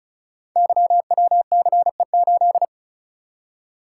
Morse code, heard as a keyed tone, about 36 wpm, 690 Hz. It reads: YWCE8
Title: Code Groups - 5 Letter:Numbers - 35wpm